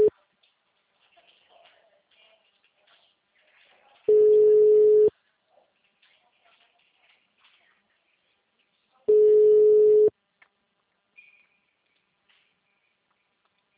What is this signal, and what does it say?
Ringback tone (Europe, 425 Hz).